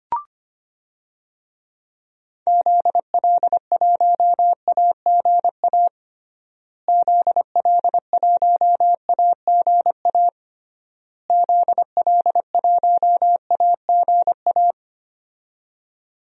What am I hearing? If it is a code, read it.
Morse code, heard as a keyed tone, about 25 wpm, 700 Hz. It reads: ZL1AGA ZL1AGA ZL1AGA
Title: DXCC-100 Call Signs - 6 Characters - Repeated 3x - 25wpm